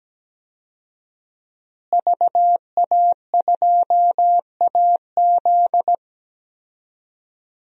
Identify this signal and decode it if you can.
Morse code, heard as a keyed tone, about 17 wpm, 700 Hz. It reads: VA2AZ
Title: DXCC-100 Call Signs - 5 Characters - Repeated 3x - 17wpm